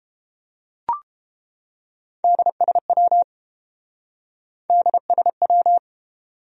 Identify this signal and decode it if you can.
Morse code, heard as a keyed tone, about 29 wpm, 700 Hz. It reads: DSW DSW